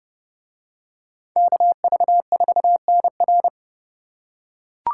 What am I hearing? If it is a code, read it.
Morse code, heard as a keyed tone, about 28 wpm, 700 Hz. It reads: KV4NR